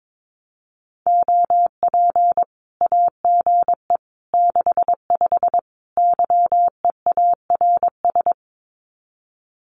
Morse code, heard as a keyed tone, about 22 wpm, 700 Hz. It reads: OP AGE 65 YEARS
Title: Common QSO Phrases 22wpm